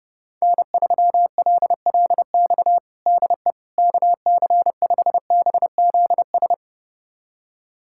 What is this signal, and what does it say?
Morse code, heard as a keyed tone, about 32 wpm, 700 Hz. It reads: N3LLX DE KC5BZS
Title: USA Callsign DE Callsign 30wpm